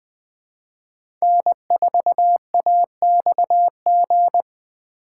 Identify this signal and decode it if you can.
Morse code, heard as a keyed tone, about 20 wpm, 700 Hz. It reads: N4AXG
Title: USA Call Signs - 5 Characters - 2x Character Spacing - 20wpm